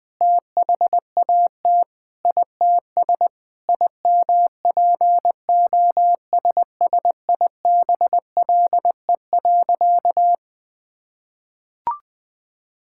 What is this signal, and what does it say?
Morse code, heard as a keyed tone, about 20 wpm, 700 Hz. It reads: THAT ITS IMPOSSIBLE.
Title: Sentences with 5-Word Phrases - 20wpm